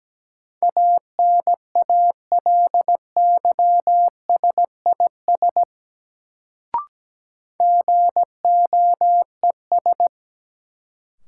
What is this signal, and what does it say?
Morse code, heard as a keyed tone, about 17 wpm, 700 Hz. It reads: ANALYSIS GOES
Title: Sentences from Top 1000 Words - Review Part 5 - New Word Drill - 17wpm